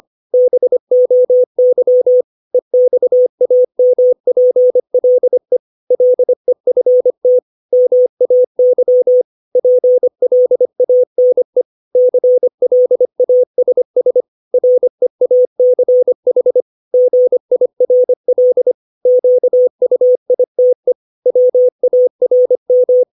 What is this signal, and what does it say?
Morse code, heard as a keyed tone, about 25 wpm, 500 Hz. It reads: BOY EXAMPLE LEFT MAY PLANE CLASS REACH GIRL QUITE WARM
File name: 632 words / 632 Words 25WPM